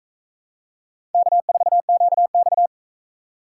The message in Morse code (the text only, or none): K4QX